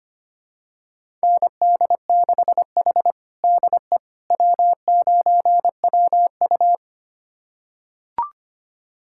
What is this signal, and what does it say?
Morse code, heard as a keyed tone, about 25 wpm, 700 Hz. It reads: ND6H DE W9WU
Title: USA Callsign DE Callsign - 4 Characters - 25wpm